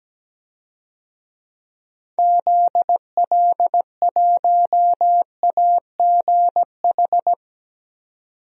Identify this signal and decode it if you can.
Morse code, heard as a keyed tone, about 17 wpm, 700 Hz. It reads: ZL1AGH
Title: DXCC-100 Call Signs - 6 Characters - 17wpm